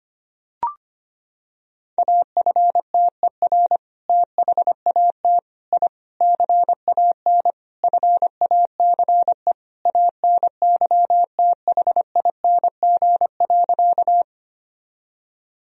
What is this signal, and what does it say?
Morse code, heard as a keyed tone, about 25 wpm, 700 Hz. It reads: AFTER THAT I CAN FACE ANYTHING.
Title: Sentences from Top 1000 Words - Review Part 2 - 25wpm